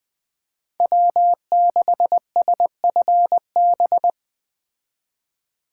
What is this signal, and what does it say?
Morse code, heard as a keyed tone, about 20 wpm, 700 Hz. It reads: W6SFB
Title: Speed Racing - USA 1x3 Call Signs 17wpm